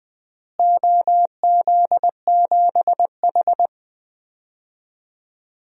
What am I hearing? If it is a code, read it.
Morse code, heard as a keyed tone, about 21 wpm, 700 Hz. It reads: OZ7H
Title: DXCC-100 Call Signs - 4 Characters - Repeated 2x - 20wpm